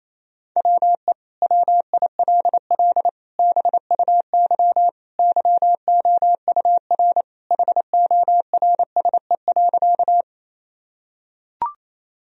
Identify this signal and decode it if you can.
Morse code, heard as a keyed tone, about 27 wpm, 700 Hz. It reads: WE WILL BUY YOUR HORSE.
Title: Sentences from Top 1000 Words - Review Part 4 - 28wpm